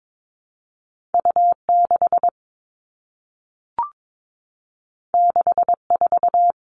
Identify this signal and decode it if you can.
Morse code, heard as a keyed tone, about 22 wpm, 700 Hz. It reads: U 6 6 4
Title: Two Challenging Letter-Number Combinations 22wpm